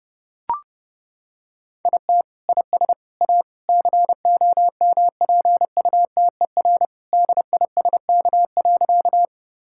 Morse code, heard as a keyed tone, about 30 wpm, 700 Hz. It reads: IT IS A COMPUTER DISK.
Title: Sentences from Top 1000 Words - Review Part 6 - 30wpm